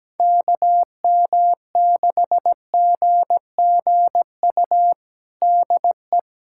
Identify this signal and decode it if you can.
Morse code, heard as a keyed tone, about 17 wpm, 700 Hz. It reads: KM6GGU DE
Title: USA Callsign DE Callsign 17wpm